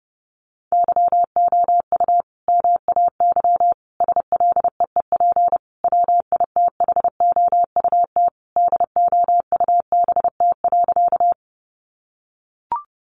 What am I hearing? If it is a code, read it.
Morse code, heard as a keyed tone, about 30 wpm, 700 Hz. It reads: YOU MAY SLEEP WITHOUT DOUBT.
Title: Sentences from Top 1500 Words 30wpm